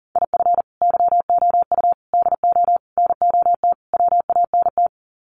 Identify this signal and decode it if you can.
Morse code, heard as a keyed tone, about 40 wpm, 700 Hz. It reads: IF YOU DO NOT WANT
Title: Sets of 5 Words 40wpm